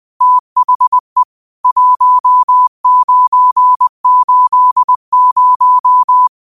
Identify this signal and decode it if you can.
Morse code, heard as a keyed tone, about 20 wpm, 1000 Hz. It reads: THE 1980
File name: qotdfiles / 20wpm